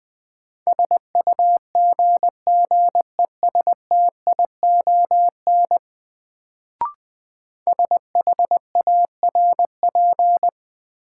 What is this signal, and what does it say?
Morse code, heard as a keyed tone, about 20 wpm, 700 Hz. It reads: SUGGESTION SHARP
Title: Sentences from Top 1600 Words - New Word Drill - 20wpm